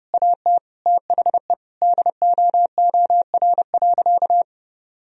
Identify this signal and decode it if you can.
Morse code, heard as a keyed tone, about 30 wpm, 700 Hz. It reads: AT THE DOOR.
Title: Sentences from Top 1000 Words - Review Part 2 - 30wpm